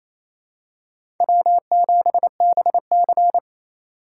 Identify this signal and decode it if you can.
Morse code, heard as a keyed tone, about 28 wpm, 700 Hz. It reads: W7BC
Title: USA Call Signs - 4 Characters - 3x Character Spacing - 28wpm